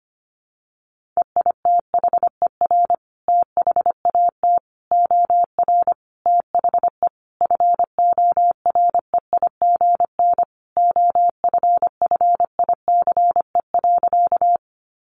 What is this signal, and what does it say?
Morse code, heard as a keyed tone, about 25 wpm, 700 Hz. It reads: EITHER THAT OR THE FOREIGN OFFICE.
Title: Sentences from Top 1400 Words 25wpm